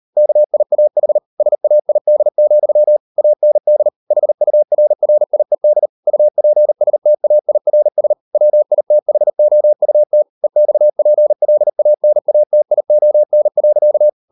Morse code, heard as a keyed tone, about 38 wpm, 600 Hz. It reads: KIAH SAID, AND HURRIED UPSTAIRS WITHOUT EXPLANATION.